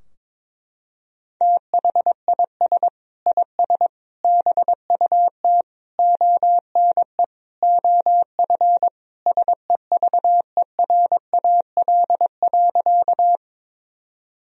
Morse code, heard as a keyed tone, about 22 wpm, 700 Hz. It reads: THIS IS BUT ONE OF SEVERAL.
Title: Sentences from Top 1000 Words - Review Part 2 - 22wpm